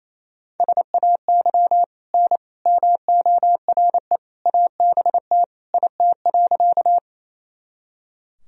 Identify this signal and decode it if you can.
Morse code, heard as a keyed tone, about 28 wpm, 700 Hz. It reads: SAY N MORE ABT IT.
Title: Sentences from Top 200 Words and Abbreviations 28wpm